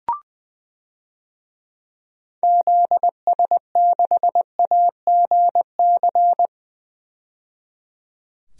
Morse code, heard as a keyed tone, about 21 wpm, 700 Hz. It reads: ZS6AGC
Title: DXCC-100 Call Signs - 6 Characters - 20wpm